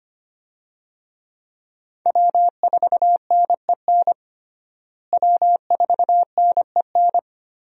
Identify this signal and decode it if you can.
Morse code, heard as a keyed tone, about 25 wpm, 700 Hz. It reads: W4NEN W4NEN
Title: USA Call Signs - 5 Characters - Repeated 2x - 25wpm